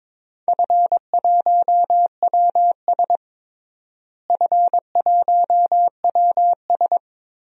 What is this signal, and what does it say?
Morse code, heard as a keyed tone, about 22 wpm, 700 Hz. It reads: F1WS F1WS
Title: DXCC-100 Call Signs - 4 Characters - Repeated 2x - 22wpm